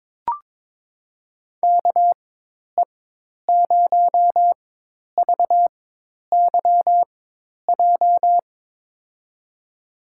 Morse code, heard as a keyed tone, about 22 wpm, spaced as for 10 wpm, 700 Hz. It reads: KE0VYJ
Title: USA Call Signs - 4x Character Spacing - 22wpm